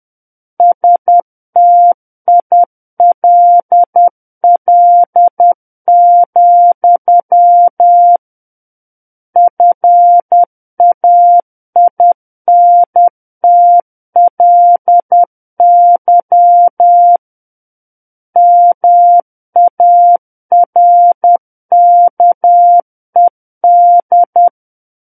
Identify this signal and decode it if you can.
Morse code, heard as a keyed tone, about 10 wpm, 700 Hz. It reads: STILL, FAINTLY MARKED
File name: Book One / 01-Chapter 1 - 10 WPM